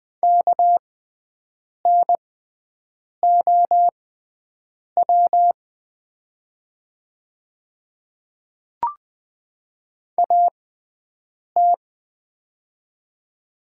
Morse code, heard as a keyed tone, about 20 wpm, spaced as for 7 wpm, 700 Hz. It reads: KNOW AT
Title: 6x Character Spacing - Top 100 Words - 20wpm